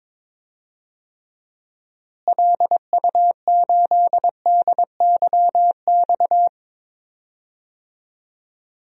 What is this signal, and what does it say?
Morse code, heard as a keyed tone, about 21 wpm, 700 Hz. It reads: LU8DYX
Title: DXCC-100 Call Signs - 6 Characters - Repeated 3x - 22wpm